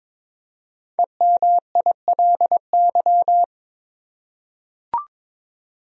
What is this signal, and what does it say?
Morse code, heard as a keyed tone, about 22 wpm, 700 Hz. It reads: EMILY
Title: USA Common Names - 22wpm